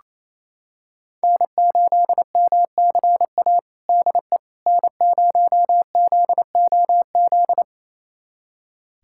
Morse code, heard as a keyed tone, about 28 wpm, 700 Hz. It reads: N8MCA DE N0ZOZ
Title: USA Callsign DE Callsign - 5 Characters - 28wpm